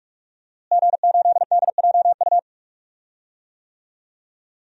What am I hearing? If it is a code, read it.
Morse code, heard as a keyed tone, about 41 wpm, 700 Hz. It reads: G8DJU
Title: DXCC-100 Call Signs - 5 Characters - Repeated 3x - 45wpm